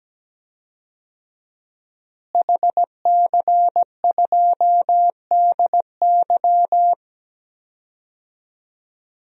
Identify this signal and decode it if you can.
Morse code, heard as a keyed tone, about 17 wpm, 700 Hz. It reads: HC2DY
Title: DXCC-100 Call Signs - 5 Characters - Repeated 3x - 17wpm